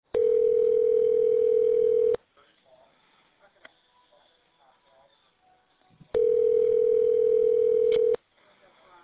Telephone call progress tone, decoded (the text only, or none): ringback tone